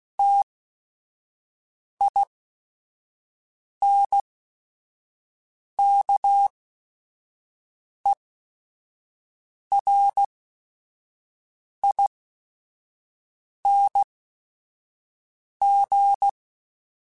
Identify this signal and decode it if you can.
Morse code, heard as a keyed tone, about 16 wpm, spaced as for 5 wpm, 800 Hz. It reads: TINKERING